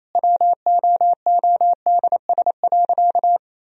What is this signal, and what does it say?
Morse code, heard as a keyed tone, about 28 wpm, 700 Hz. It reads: WOODS.
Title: Sentences from Top 1000 Words - Review Part 2 - 28wpm